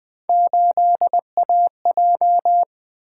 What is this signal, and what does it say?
Morse code, heard as a keyed tone, about 24 wpm, 700 Hz. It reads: 8AJ
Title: Speed-Racing - Three Letter-Number Combinations - 20wpm